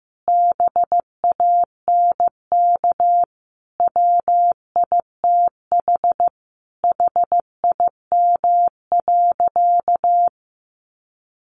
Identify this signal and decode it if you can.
Morse code, heard as a keyed tone, about 15 wpm, 700 Hz. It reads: BANK WITH HIM.
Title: Sentences from Top 900 Words 15wpm